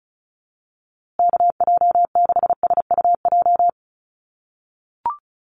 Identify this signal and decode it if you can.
Morse code, heard as a keyed tone, about 34 wpm, 700 Hz. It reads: KJ6SUJ
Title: USA Call Signs Repeated 2x 35wpm